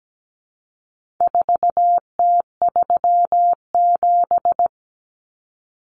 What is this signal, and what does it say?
Morse code, heard as a keyed tone, about 17 wpm, 700 Hz. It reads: 4T37
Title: Common Cut Numbers - 4-Digit 17wpm